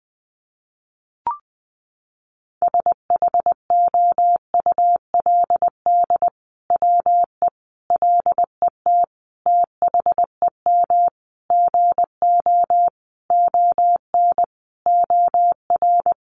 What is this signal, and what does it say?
Morse code, heard as a keyed tone, about 19 wpm, 700 Hz. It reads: SHOULD WE LET THEM GO ON OR
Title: Sentences from Top 600 Words 20wpm